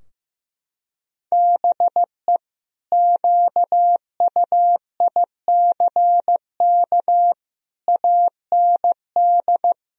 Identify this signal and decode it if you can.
Morse code, heard as a keyed tone, about 15 wpm, 700 Hz. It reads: BE QUICK AND
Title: Sentences from Top 1000 Words - Review Part 2 - 15wpm